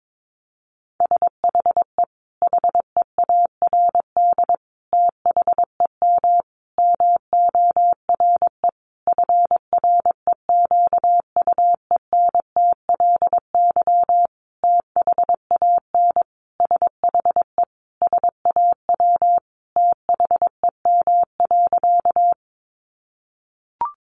Morse code, heard as a keyed tone, about 22 wpm, 700 Hz. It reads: SHE HEARD THEM MORE FREQUENTLY THAN SHE SAW THEM.
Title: Sentences from Top 1300 Words 22wpm